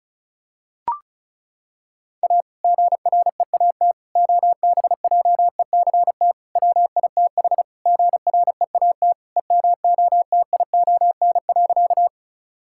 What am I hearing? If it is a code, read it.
Morse code, heard as a keyed tone, about 36 wpm, 700 Hz. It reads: A GREAT OBJECT WITH GREAT EMOTION.
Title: Sentences from Top 1600 Words 35wpm